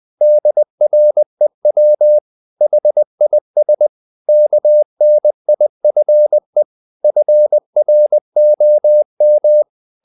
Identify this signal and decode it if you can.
Morse code, heard as a keyed tone, about 20 wpm, 600 Hz. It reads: DREW HIS KNIFE FROM
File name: the old man and the sea part 2 20wpm